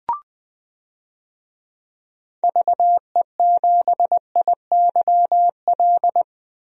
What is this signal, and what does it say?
Morse code, heard as a keyed tone, about 21 wpm, 700 Hz. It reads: VE7IYL
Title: DXCC-100 Call Signs - 6 Characters - Repeated 3x - 20wpm